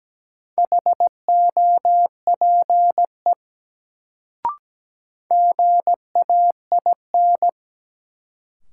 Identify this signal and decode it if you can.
Morse code, heard as a keyed tone, about 17 wpm, 700 Hz. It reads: HOPE GAIN
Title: Sentences from Top 1000 Words - New Word Drill - 17wpm